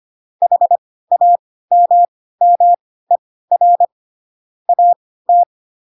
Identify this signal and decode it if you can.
Morse code, heard as a keyed tone, about 25 wpm, spaced as for 16 wpm, 700 Hz. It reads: HAMMER AT
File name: Pinocchio / Pinocchio ch7 10 25 ik1hge